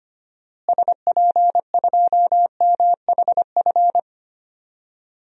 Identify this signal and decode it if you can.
Morse code, heard as a keyed tone, about 25 wpm, 700 Hz. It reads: SP2MHF
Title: DXCC-100 Call Signs - 6 Characters - Repeated 2x - 25wpm